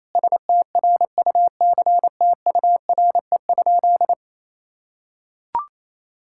Morse code, heard as a keyed tone, about 28 wpm, 700 Hz. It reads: STRUCTURE?
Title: Sentences from Top 1000 Words - Review Part 6 - 28wpm